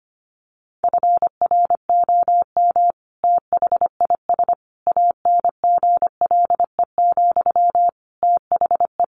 Morse code, heard as a keyed tone, about 25 wpm, 700 Hz. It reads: FROM THIS ANGLE, THE
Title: Sentences from Top 1500 Words 25wpm